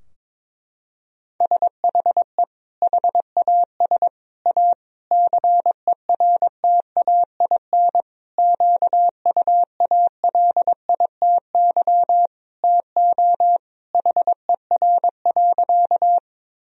Morse code, heard as a keyed tone, about 22 wpm, 700 Hz. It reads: SHE HAS A CERTAIN QUALITY TO HER.
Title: Sentences from Top 1000 Words - Review Part 4 - 22wpm